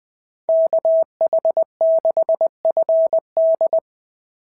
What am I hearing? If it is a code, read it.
Morse code, heard as a keyed tone, about 20 wpm, 690 Hz. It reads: KH6FD
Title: SOTA QSO Practice 20wpm